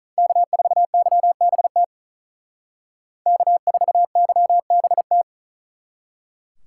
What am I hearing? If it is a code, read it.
Morse code, heard as a keyed tone, about 45 wpm, 700 Hz. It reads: K4YBT K4YBT